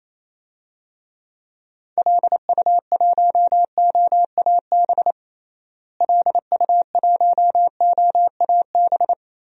Morse code, heard as a keyed tone, about 28 wpm, 700 Hz. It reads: LU1OAB LU1OAB
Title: DXCC-100 Call Signs - 6 Characters - Repeated 3x - 28wpm